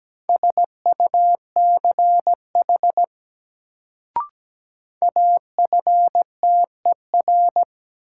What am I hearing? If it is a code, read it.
Morse code, heard as a keyed tone, about 17 wpm, 700 Hz. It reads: SUCH AFTER
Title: Sentences from Top 200 Words - New Word Drill - 17wpm